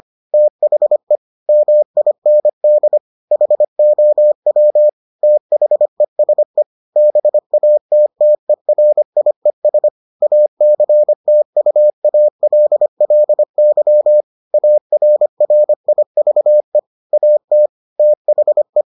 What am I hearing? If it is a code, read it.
Morse code, heard as a keyed tone, about 25 wpm, 600 Hz. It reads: THE MIND HOW THESE BATTERIES ACTUALLY ARRIVE AT THE